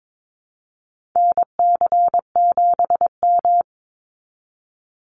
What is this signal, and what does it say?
Morse code, heard as a keyed tone, about 22 wpm, 700 Hz. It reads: NC7M
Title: Call Signs Repeated 3x - 22wpm